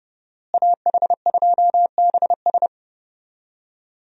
Morse code, heard as a keyed tone, about 29 wpm, 700 Hz. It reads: AH2BS
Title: USA Call Signs - 5 Characters - Repeated 3x - 30wpm